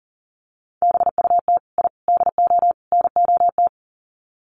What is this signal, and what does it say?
Morse code, heard as a keyed tone, about 40 wpm, 700 Hz. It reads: BUT I DO NOT
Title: Sets of 4 Words 40wpm